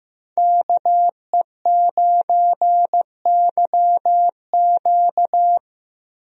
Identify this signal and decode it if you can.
Morse code, heard as a keyed tone, about 15 wpm, 700 Hz. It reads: KE9YQ
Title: USA Call Signs - 5 Characters - Repeated 2x - 15wpm